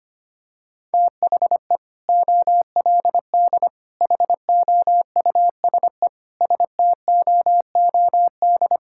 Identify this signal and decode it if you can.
Morse code, heard as a keyed tone, about 25 wpm, 700 Hz. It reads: THE OLD HOUSE STOOD
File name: practice / 25wpm / Sentences with 2-Word Phrases - 25wpm